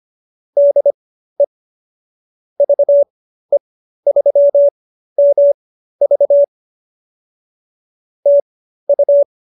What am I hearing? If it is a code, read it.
Morse code, heard as a keyed tone, about 25 wpm, spaced as for 13 wpm, 570 Hz. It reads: DE VE3MV TU